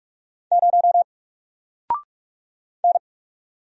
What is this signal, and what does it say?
Morse code, heard as a keyed tone, about 44 wpm, 700 Hz. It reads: 0 N
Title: Single Letter-Number Alt 45wpm